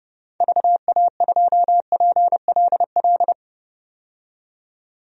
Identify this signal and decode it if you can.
Morse code, heard as a keyed tone, about 30 wpm, 700 Hz. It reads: VA2PLL
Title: DXCC-100 Call Signs - 6 Characters - 30wpm